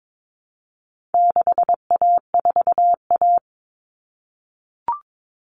Morse code, heard as a keyed tone, about 22 wpm, 700 Hz. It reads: 6A4A
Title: Common Cut Numbers - 4-Digit 22wpm